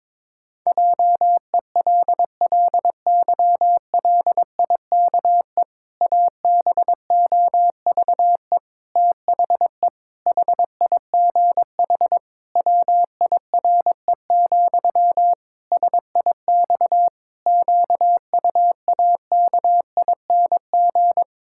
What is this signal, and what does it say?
Morse code, heard as a keyed tone, about 22 wpm, 700 Hz. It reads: JELLYLIKE ABOVE THE HIGH WIRE, SIX QUAKING
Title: Pangram Challenge v2 - 22wpm